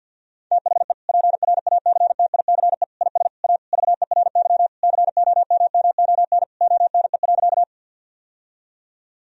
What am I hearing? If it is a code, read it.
Morse code, heard as a keyed tone, about 52 wpm, 700 Hz. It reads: THE PRACTICE IS A VERY COMMON ONE.
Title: Sentences from Top 1000 Words - Review Part 4 - 50wpm